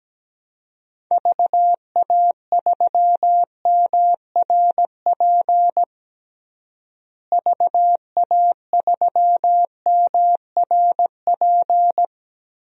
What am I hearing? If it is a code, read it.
Morse code, heard as a keyed tone, about 17 wpm, 700 Hz. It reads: VA3MRP VA3MRP
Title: DXCC-100 Call Signs - 6 Characters - Repeated 2x - 17wpm